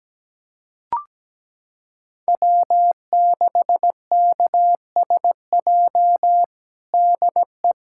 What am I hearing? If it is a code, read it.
Morse code, heard as a keyed tone, about 17 wpm, 700 Hz. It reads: W6KSJ DE
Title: USA Callsign DE Callsign - 5 Characters - 17wpm